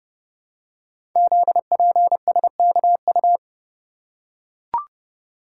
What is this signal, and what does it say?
Morse code, heard as a keyed tone, about 30 wpm, 700 Hz. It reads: ZPSKU
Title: Code Groups - 5 Letter:Numbers - 30wpm